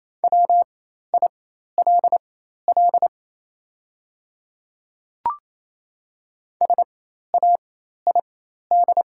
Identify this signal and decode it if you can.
Morse code, heard as a keyed tone, about 28 wpm, spaced as for 13 wpm, 700 Hz. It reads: WILL SAID
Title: 4x Character Spacing - Top 100 Words - 28wpm